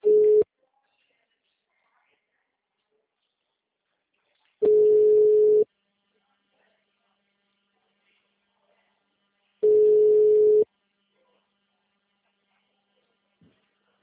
Ringback tone (Europe, 425 Hz).
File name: out- -2019- - - .78